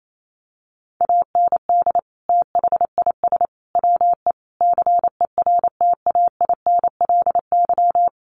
Morse code, heard as a keyed tone, about 28 wpm, 700 Hz. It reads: AND THIS WE CERTAINLY
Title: Sentences from Top 900 Words 28wpm